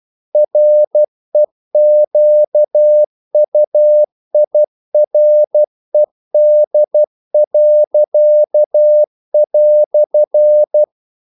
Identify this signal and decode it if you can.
Morse code, heard as a keyed tone, about 12 wpm, 600 Hz. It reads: REQUIRED."
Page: English quotations in CW (telegraphy - morse)